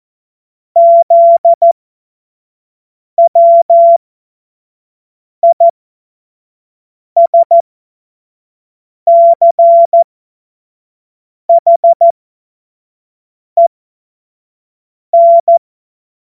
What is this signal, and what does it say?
Morse code, heard as a keyed tone, about 14 wpm, spaced as for 5 wpm, 680 Hz. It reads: ZWISCHEN